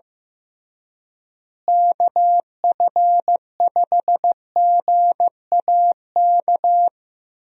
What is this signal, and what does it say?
Morse code, heard as a keyed tone, about 15 wpm, 700 Hz. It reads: KF5GAK
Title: USA Call Signs - 6 Characters - Repeated 3x - 15wpm